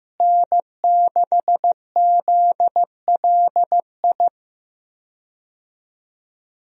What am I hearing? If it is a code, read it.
Morse code, heard as a keyed tone, about 15 wpm, 700 Hz. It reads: N6ZLI
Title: USA Call Signs - 5 Characters - Repeated 2x - 15wpm